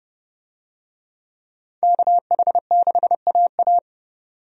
Morse code, heard as a keyed tone, about 30 wpm, 700 Hz. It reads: KH6AA
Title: DXCC-100 Call Signs - 5 Characters - 30wpm